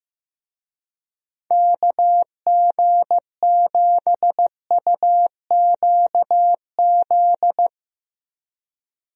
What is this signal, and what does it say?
Morse code, heard as a keyed tone, about 15 wpm, 700 Hz. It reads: KG7UQZ
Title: USA Call Signs - 6 Characters - Repeated 2x - 15wpm